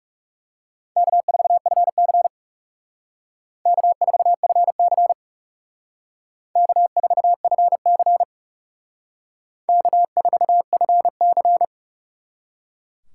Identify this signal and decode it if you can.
Morse code, heard as a keyed tone, about 36 wpm, 700 Hz. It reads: K4FC K4FC K4FC K4FC
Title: Speed Racing - USA 1x2 Call Signs 30wpm